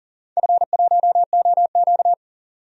Morse code, heard as a keyed tone, about 40 wpm, 700 Hz. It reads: F1OQ
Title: DXCC-100 Call Signs - 4 Characters - 40wpm